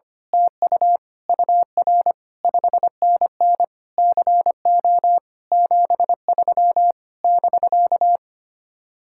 Morse code, heard as a keyed tone, about 24 wpm, 700 Hz. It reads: TU UR 5NN CO 73 <BK>
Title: Speed-Racing Common QSO Phrases Encore - 25wpm